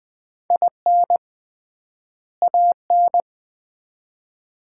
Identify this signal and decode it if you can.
Morse code, heard as a keyed tone, about 21 wpm, 700 Hz. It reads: IN AN
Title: 3x Word-Spacing - Sets of 2 Words - 20wpm